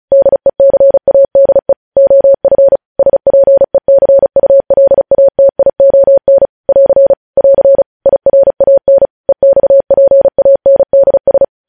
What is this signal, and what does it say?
Morse code, heard as a keyed tone, about 36 wpm, 550 Hz. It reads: DECADE OF SPECULATION <AR> <AR> IRAN EXPANDS
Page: news headlines in morse code at 35 wpm